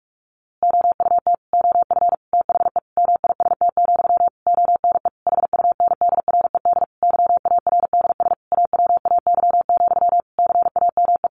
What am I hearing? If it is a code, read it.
Morse code, heard as a keyed tone, about 44 wpm, 700 Hz. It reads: OUT OF THE MIST, ONE HUNDRED YARDS AWAY, CAME
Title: A Sound Of Thunder 45wpm